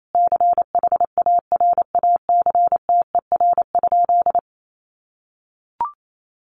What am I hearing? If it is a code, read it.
Morse code, heard as a keyed tone, about 28 wpm, 700 Hz. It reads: CHARACTER?
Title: Sentences from Top 1200 Words 28wpm